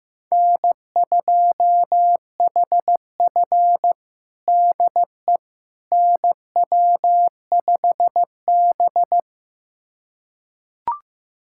Morse code, heard as a keyed tone, about 15 wpm, 700 Hz. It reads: N2HF DE NW5B
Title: USA Callsign DE Callsign - 4 Characters - 15wpm